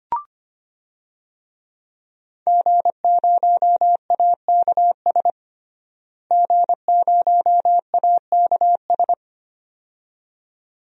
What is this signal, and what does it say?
Morse code, heard as a keyed tone, about 25 wpm, 700 Hz. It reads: G0AKS G0AKS
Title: DXCC-100 Call Signs - 5 Characters - Repeated 2x - 25wpm